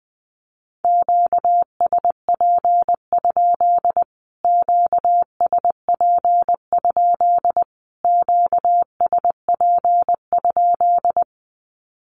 Morse code, heard as a keyed tone, about 20 wpm, 700 Hz. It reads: QSP? QSP? QSP?
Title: Q-Codes Repeated 3x 20wpm